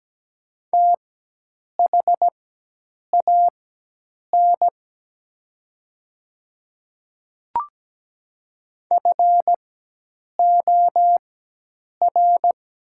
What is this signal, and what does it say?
Morse code, heard as a keyed tone, about 17 wpm, spaced as for 8 wpm, 700 Hz. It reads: THAN FOR
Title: 4x Character Spacing - Top 100 Words - 17wpm